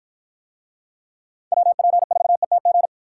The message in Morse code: WP4ETG

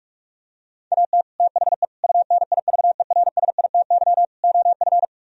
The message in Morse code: AT THE UNIVERSITY OF

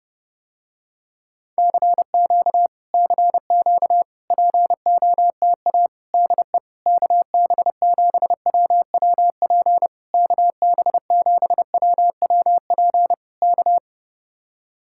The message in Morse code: CQ CQ POTA DE KB7WWP KB7WWP K

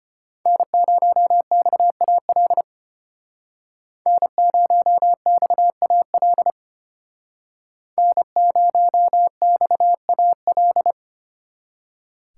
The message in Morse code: N0XAL N0XAL N0XAL

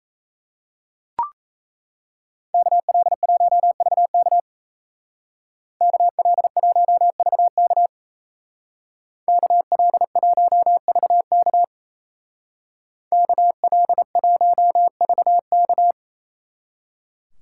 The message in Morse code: KL1VK KL1VK KL1VK KL1VK

